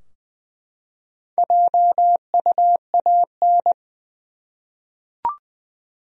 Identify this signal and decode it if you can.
Morse code, heard as a keyed tone, about 20 wpm, 700 Hz. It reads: JUAN
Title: USA Common Names - Max 4 Letters - 20wpm